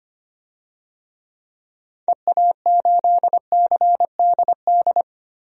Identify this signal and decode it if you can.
Morse code, heard as a keyed tone, about 25 wpm, 700 Hz. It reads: EA8CDD
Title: DXCC-100 Call Signs - 6 Characters - Repeated 3x - 25wpm